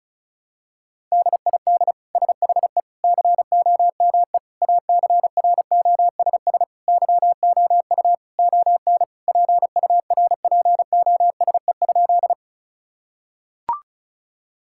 Morse code, heard as a keyed tone, about 34 wpm, 700 Hz. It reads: DID SHE COME ACROSS YOU ON PURPOSE?
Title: Sentences from Top 1000 Words - Review Part 5 - 35wpm